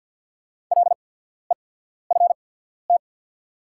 Morse code, heard as a keyed tone, about 48 wpm, spaced as for 14 wpm, 690 Hz. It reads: LEFT
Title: 8x Character Spacing - Top 500 Words - 50wpm